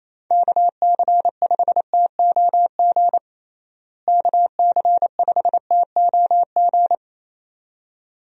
Morse code, heard as a keyed tone, about 28 wpm, 700 Hz. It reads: KC5TOG KC5TOG
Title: USA Call Signs - 6 Characters - Repeated 2x - 28wpm